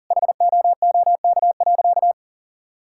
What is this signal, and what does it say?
Morse code, heard as a keyed tone, about 40 wpm, 700 Hz. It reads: HOOK.
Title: Sentences from Top 1600 Words 40wpm